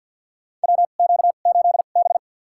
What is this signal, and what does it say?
Morse code, heard as a keyed tone, about 49 wpm, spaced as for 35 wpm, 700 Hz. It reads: WQ8B
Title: USA Call Signs - 2x Character Spacing - 50wpm